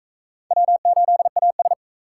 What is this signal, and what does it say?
Morse code, heard as a keyed tone, about 41 wpm, 700 Hz. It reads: W8AS